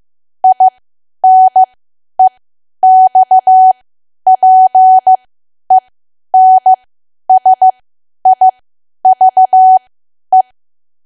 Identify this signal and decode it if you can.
Morse code, heard as a keyed tone, about 15 wpm, spaced as for 10 wpm, 750 Hz. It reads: INEXPENSIVE
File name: cwa / W1AW-Level2 / 10WPM